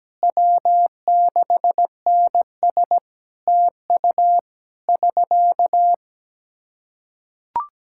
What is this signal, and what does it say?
Morse code, heard as a keyed tone, about 17 wpm, 700 Hz. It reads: W6NS TU <SK>
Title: Practice QSO - Exchange 4 - Ending - 17wpm